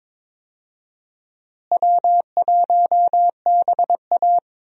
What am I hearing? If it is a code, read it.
Morse code, heard as a keyed tone, about 22 wpm, 700 Hz. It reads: W1BA
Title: USA Call Signs - 4 Characters - Repeated 2x - 22wpm